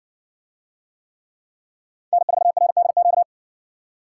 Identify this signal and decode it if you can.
Morse code, heard as a keyed tone, about 60 wpm, 700 Hz. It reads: N4RDQ